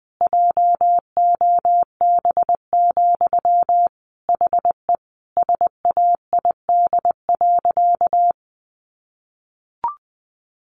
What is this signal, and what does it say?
Morse code, heard as a keyed tone, about 20 wpm, 700 Hz. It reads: JOB, HE SAID.
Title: Sentences from Top 1500 Words 20wpm